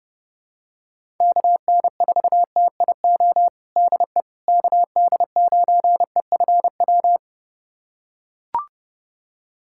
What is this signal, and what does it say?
Morse code, heard as a keyed tone, about 29 wpm, 700 Hz. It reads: KN4TIO DE KD9EFW
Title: USA Callsign DE Callsign - 6 Characters - 30wpm